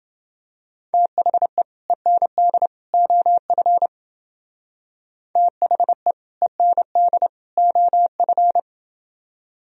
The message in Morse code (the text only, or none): THE END OF THE END OF